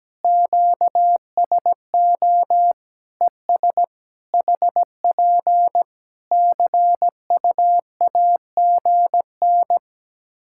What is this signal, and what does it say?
Morse code, heard as a keyed tone, about 17 wpm, 700 Hz. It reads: QSO ES HP CUAGN
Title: Practice QSO - Exchange 4 - Ending - 17wpm